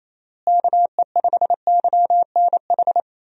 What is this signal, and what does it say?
Morse code, heard as a keyed tone, about 28 wpm, 700 Hz. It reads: KE5YNH
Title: USA Call Signs - 6 Characters - Repeated 3x - 28wpm